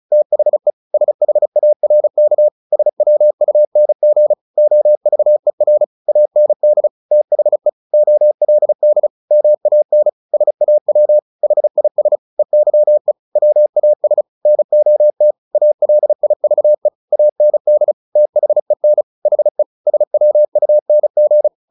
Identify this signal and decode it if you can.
Morse code, heard as a keyed tone, about 34 wpm, 600 Hz. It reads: THE SHARK SWUNG OVER AND THE OLD MAN SAW HIS EYE WAS NOT ALIVE AND THEN HE SWUNG
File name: the old man and the sea 2 35wpm